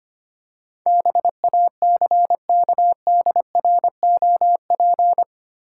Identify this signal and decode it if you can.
Morse code, heard as a keyed tone, about 25 wpm, 700 Hz. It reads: BACKDROP
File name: 8 Letter Words - First 3 Letters Given - 25wpm